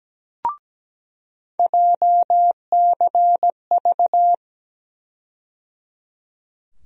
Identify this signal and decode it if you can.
Morse code, heard as a keyed tone, about 17 wpm, 700 Hz. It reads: JCV
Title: USA Call Sign Postfixes - 3 Letters 17wpm